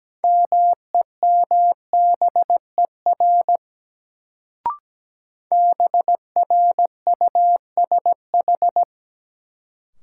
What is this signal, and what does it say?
Morse code, heard as a keyed tone, about 17 wpm, 700 Hz. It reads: MEMBER BRUSH
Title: Sentences from Top 1100 Words - New Word Drill - 17wpm